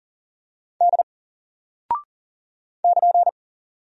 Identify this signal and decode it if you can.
Morse code, heard as a keyed tone, about 39 wpm, 700 Hz. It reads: D <KN>
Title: Single Character (Letter:Number:Punctuation:Prosign) 40wpm